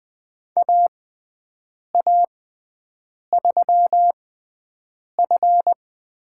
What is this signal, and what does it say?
Morse code, heard as a keyed tone, about 20 wpm, spaced as for 7 wpm, 700 Hz. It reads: AA3F
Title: USA Call Signs - 5 Characters - 6x Character Spacing - 20wpm